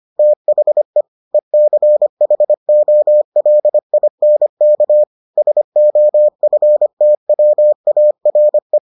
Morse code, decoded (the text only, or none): THE ECHOLINK SOFTWARE